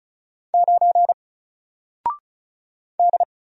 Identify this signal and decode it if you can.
Morse code, heard as a keyed tone, about 36 wpm, 700 Hz. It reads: 9 D
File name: Single-Letter-Number-35wpm